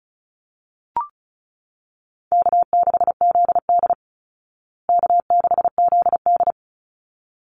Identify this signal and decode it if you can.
Morse code, heard as a keyed tone, about 34 wpm, 700 Hz. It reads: K6ZD K6ZD
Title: Easy Call Signs Repeated 2x - 35wpm